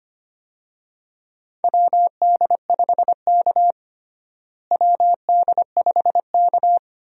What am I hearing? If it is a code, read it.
Morse code, heard as a keyed tone, about 25 wpm, 700 Hz. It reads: WD5K WD5K
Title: USA Call Signs - 4 Characters - Repeated 2x - 25wpm